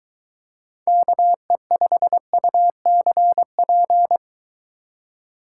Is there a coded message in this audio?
Yes: Morse code from 0.9 to 4.2 s, heard as a keyed tone, about 22 wpm, 700 Hz. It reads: KE5UCP